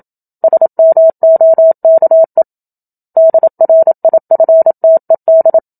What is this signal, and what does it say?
Morse code, heard as a keyed tone, about 27 wpm, 650 Hz. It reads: SMOKE DRIFTED
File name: Book Two / 01-Chapter 1 - 27 WPM